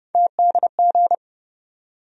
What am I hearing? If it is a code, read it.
Morse code, heard as a keyed tone, about 30 wpm, 700 Hz. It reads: TDG
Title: USA Call Sign Postfixes - 3 Letters 30wpm